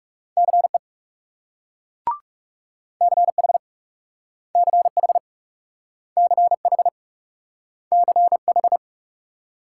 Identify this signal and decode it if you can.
Morse code, heard as a keyed tone, about 38 wpm, 700 Hz. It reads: CE CH CH CH CH